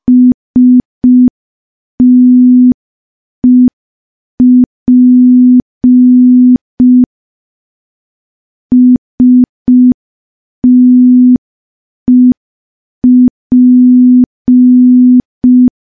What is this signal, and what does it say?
Morse code, heard as a keyed tone, about 5 wpm, 250 Hz. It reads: STEP STEP
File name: morse / morse-training-05WPM